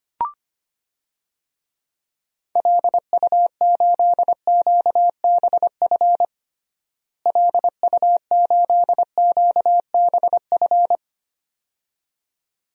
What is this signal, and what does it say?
Morse code, heard as a keyed tone, about 25 wpm, 700 Hz. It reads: LU8QBF LU8QBF
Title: DXCC-100 Call Signs - 6 Characters - Repeated 2x - 25wpm